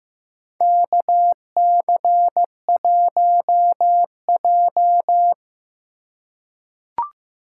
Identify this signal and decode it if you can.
Morse code, heard as a keyed tone, about 15 wpm, 700 Hz. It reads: KC1J
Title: USA Call Signs - 4 Characters - 3x Character Spacing - 15wpm